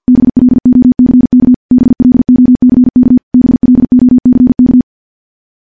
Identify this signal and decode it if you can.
Morse code, heard as a keyed tone, about 50 wpm, 250 Hz. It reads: BLOCK BLOCK BLOCK
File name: morse / morse-training-50WPM